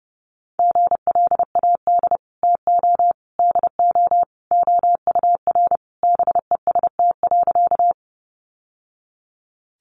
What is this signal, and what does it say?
Morse code, heard as a keyed tone, about 28 wpm, 700 Hz. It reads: GLAD TO DO OUR BEST.
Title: Sentences from Top 800 Words 30wpm